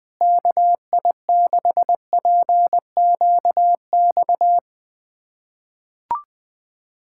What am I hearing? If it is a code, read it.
Morse code, heard as a keyed tone, about 21 wpm, 700 Hz. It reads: KI6PQX
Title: USA Callsign DE Callsign - 6 Characters - 20wpm